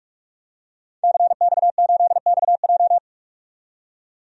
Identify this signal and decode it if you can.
Morse code, heard as a keyed tone, about 42 wpm, 700 Hz. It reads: CX8XJ
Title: DXCC-100 Call Signs - 5 Characters - 45wpm